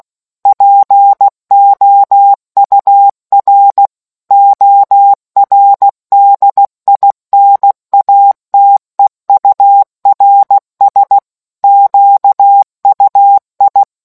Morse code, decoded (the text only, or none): POUR ORDINATEURS QUI